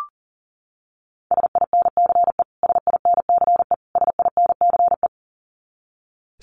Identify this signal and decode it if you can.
Morse code, heard as a keyed tone, about 40 wpm, 700 Hz. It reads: SINCE SINCE SINCE
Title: Top 500 Words Repeated 40wpm